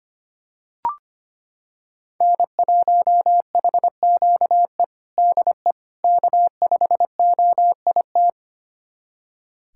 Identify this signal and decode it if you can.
Morse code, heard as a keyed tone, about 25 wpm, 700 Hz. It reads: N1HQE DE K5OIT
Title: USA Callsign DE Callsign - 5 Characters - 25wpm